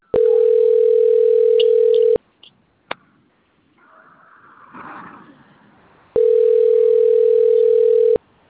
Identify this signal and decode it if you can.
Ringback tone (North America), 440 + 480 Hz.
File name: out- - - - - .60295